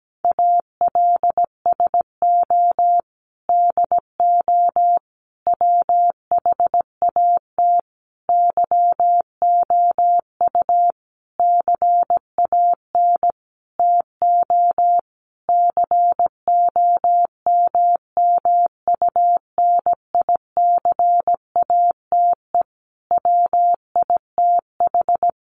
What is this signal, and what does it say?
Morse code, heard as a keyed tone, about 17 wpm, 700 Hz. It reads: ALSO DO WHAT YOU CAN TO COMMUNICATE WITH
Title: Sentences from Top 1300 Words 17wpm